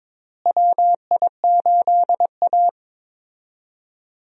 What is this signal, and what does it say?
Morse code, heard as a keyed tone, about 22 wpm, 700 Hz. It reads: WI8A
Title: USA Call Signs - 4 Characters - Repeated 3x - 22wpm